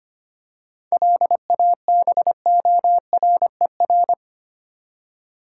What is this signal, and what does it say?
Morse code, heard as a keyed tone, about 25 wpm, 700 Hz. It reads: LABORER
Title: Practice QSO - Exchange 3 - More Info - 25wpm